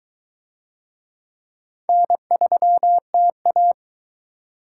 Morse code, heard as a keyed tone, about 22 wpm, 700 Hz. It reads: N3TA